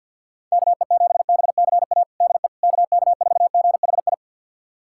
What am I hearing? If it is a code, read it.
Morse code, heard as a keyed tone, about 50 wpm, 700 Hz. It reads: KE7BCA DE KK4GHI
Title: USA Callsign DE Callsign 50wpm